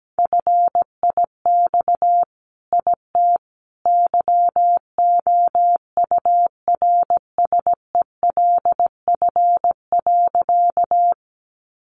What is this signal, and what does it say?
Morse code, heard as a keyed tone, about 17 wpm, 700 Hz. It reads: FIX IT YOURSELF.
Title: Sentences from Top 1400 Words 17wpm